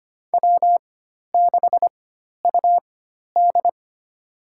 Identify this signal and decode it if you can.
Morse code, heard as a keyed tone, about 25 wpm, spaced as for 12 wpm, 700 Hz. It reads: W6UD
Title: USA Call Signs - 4 Characters - 4x Character Spacing - 25wpm